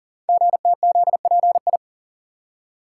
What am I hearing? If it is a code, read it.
Morse code, heard as a keyed tone, about 44 wpm, 700 Hz. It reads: GTZPI
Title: Code Groups - 5 Letters - 40wpm